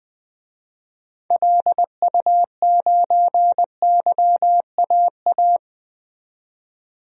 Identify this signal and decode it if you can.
Morse code, heard as a keyed tone, about 20 wpm, 700 Hz. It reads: LU9YAA
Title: DXCC-100 Call Signs - 6 Characters - 20wpm